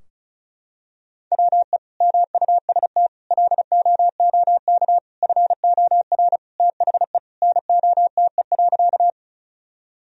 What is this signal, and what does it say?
Morse code, heard as a keyed tone, about 36 wpm, 700 Hz. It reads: WE MUST LOOK FOR THE NOTE.
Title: Sentences from Top 1000 Words - Review Part 2 - 35wpm